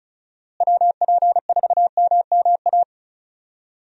Morse code, heard as a keyed tone, about 34 wpm, 700 Hz. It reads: WP4MMA
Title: USA Call Signs - 6 Characters - Repeated 2x - 35wpm